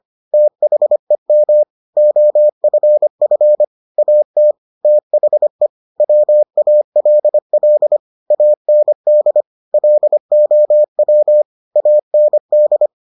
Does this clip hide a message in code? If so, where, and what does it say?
Yes: Morse code from 0.3 to 12.9 s, heard as a keyed tone, about 25 wpm, 600 Hz. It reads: THEM OFF AT THE WALL AND LOW AND